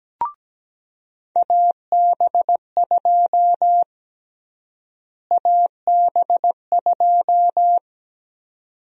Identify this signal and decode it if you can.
Morse code, heard as a keyed tone, about 17 wpm, 700 Hz. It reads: AB2 AB2
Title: USA Call Sign Prefixes - Repeated 2x - 17wpm